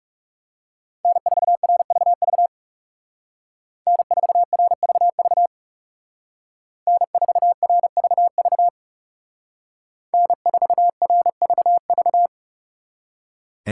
Morse code, heard as a keyed tone, about 39 wpm, 700 Hz. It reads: N4RVV N4RVV N4RVV N4RVV